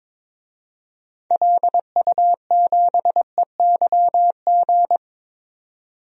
Morse code, heard as a keyed tone, about 22 wpm, 700 Hz. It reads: LU7EYG
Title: DXCC-100 Call Signs - 6 Characters - 22wpm